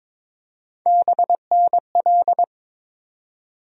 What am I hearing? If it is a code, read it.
Morse code, heard as a keyed tone, about 21 wpm, 700 Hz. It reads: BNL
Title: USA Call Sign Postfixes - 3 Letters 22wpm